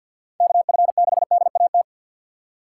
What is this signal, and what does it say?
Morse code, heard as a keyed tone, about 53 wpm, 700 Hz. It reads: KF6DAT
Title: USA Call Signs - 3x Character Spacing - 50wpm